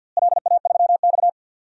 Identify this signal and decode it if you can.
Morse code, heard as a keyed tone, about 51 wpm, 700 Hz. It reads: LA3X
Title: DXCC-100 Call Signs - 4 Characters - Repeated 2x - 50wpm